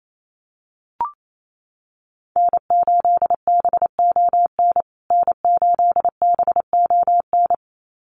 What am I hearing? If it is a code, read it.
Morse code, heard as a keyed tone, about 28 wpm, 700 Hz. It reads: N8BON N8BON
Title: USA Call Signs Repeated 2x 28wpm